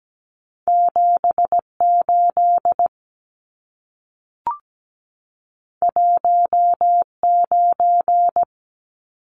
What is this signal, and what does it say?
Morse code, heard as a keyed tone, about 17 wpm, 700 Hz. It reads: 78 19
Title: Numbers - 2-Digit 17wpm